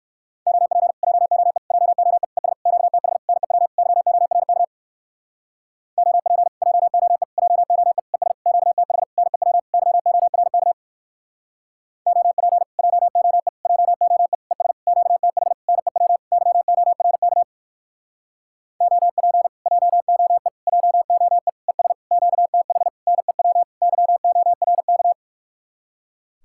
Morse code, heard as a keyed tone, about 59 wpm, 700 Hz. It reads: OP JOE JOE ES QTH NEW YORK OP JOE JOE ES QTH NEW YORK OP JOE JOE ES QTH NEW YORK OP JOE JOE ES QTH NEW YORK